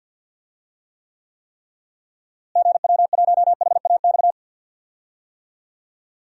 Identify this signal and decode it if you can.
Morse code, heard as a keyed tone, about 51 wpm, 700 Hz. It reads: GW1HAX
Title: DXCC-100 Call Signs - 6 Characters - Repeated 3x - 50wpm